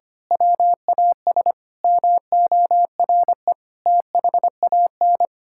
Morse code, heard as a keyed tone, about 25 wpm, 700 Hz. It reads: WAS MORE THAN
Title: Sentences from Top 1600 Words 25wpm